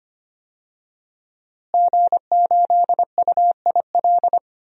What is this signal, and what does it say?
Morse code, heard as a keyed tone, about 25 wpm, 700 Hz. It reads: G8UIL
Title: DXCC-100 Call Signs - 5 Characters - Repeated 3x - 25wpm